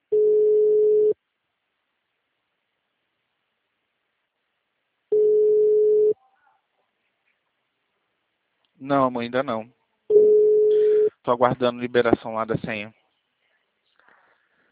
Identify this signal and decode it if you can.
Ringback tone (Europe, 425 Hz), with speech.